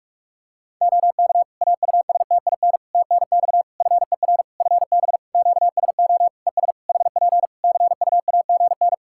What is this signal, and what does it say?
Morse code, heard as a keyed tone, about 41 wpm, 700 Hz. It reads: OK AUSTIN TNX FER FB QSO ES HP CUAGN
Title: Practice QSO - Exchange 4 - Ending - 45wpm